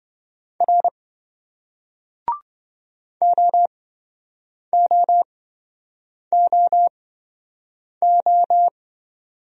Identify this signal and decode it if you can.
Morse code, heard as a keyed tone, about 26 wpm, spaced as for 7 wpm, 700 Hz. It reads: R OOOO